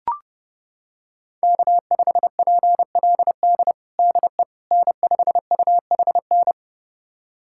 Morse code, heard as a keyed tone, about 30 wpm, 700 Hz. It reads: K5PLD DE N5UHN
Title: USA Callsign DE Callsign - 5 Characters - 30wpm